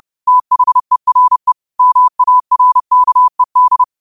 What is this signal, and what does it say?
Morse code, heard as a keyed tone, about 30 wpm, 1000 Hz. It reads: THERE MARKED
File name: qotdfiles / 30wpm